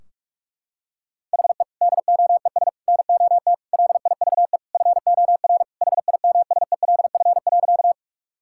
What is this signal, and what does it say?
Morse code, heard as a keyed tone, about 47 wpm, 700 Hz. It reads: HE DOES NOT LIVE FOR HIMSELF.
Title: Sentences from Top 1600 Words 45wpm